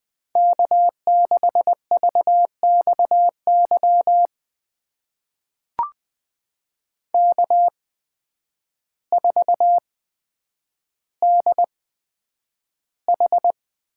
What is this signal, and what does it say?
Morse code, heard as a keyed tone, about 20 wpm, spaced as for 9 wpm, 700 Hz. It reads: K6VXY K4DH
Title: USA Call Signs - 8x Character Spacing - 20wpm